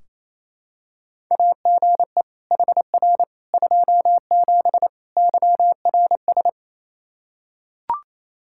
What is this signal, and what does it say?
Morse code, heard as a keyed tone, about 27 wpm, 700 Hz. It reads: AGE HR 27 YRS
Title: Practice QSO - Exchange 3 - More Info - 28wpm